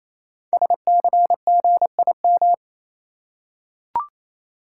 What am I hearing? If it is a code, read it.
Morse code, heard as a keyed tone, about 29 wpm, 700 Hz. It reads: SCGIM
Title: Code Groups - 5 Letter:Numbers - 28wpm